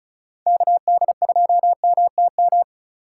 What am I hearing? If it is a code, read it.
Morse code, heard as a keyed tone, about 36 wpm, 700 Hz. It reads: KD2MTM
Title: USA Call Signs - 6 Characters - 4x Character Spacing - 35wpm